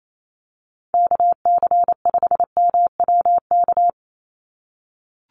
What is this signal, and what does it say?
Morse code, heard as a keyed tone, about 28 wpm, 700 Hz. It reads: KC5MWK
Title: USA Call Signs 28wpm